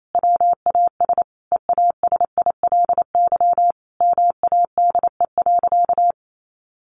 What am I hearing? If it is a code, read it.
Morse code, heard as a keyed tone, about 28 wpm, 700 Hz. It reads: WAS EASILY MADE.
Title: Sentences from Top 1200 Words 28wpm